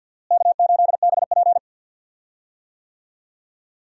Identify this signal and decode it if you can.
Morse code, heard as a keyed tone, about 46 wpm, 700 Hz. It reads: K8BP
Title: USA Call Signs - 4 Characters - Repeated 3x - 50wpm